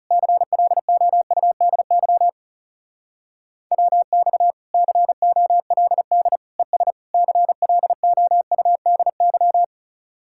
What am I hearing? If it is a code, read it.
Morse code, heard as a keyed tone, about 37 wpm, 700 Hz. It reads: CLOUDY WX COLD ES CLOUDY
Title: Speed-Racing Common QSO Phrases Encore - 30wpm